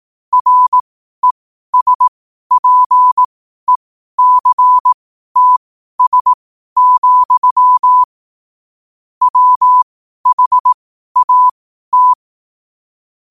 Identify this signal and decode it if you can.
Morse code, heard as a keyed tone, about 18 wpm, spaced as for 12 wpm, 1000 Hz. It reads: RESPECTS, WHAT